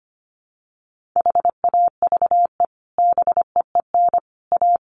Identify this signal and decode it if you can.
Morse code, heard as a keyed tone, about 25 wpm, 700 Hz. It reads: HAVE BEEN A
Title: Sets of 3 Words 25wpm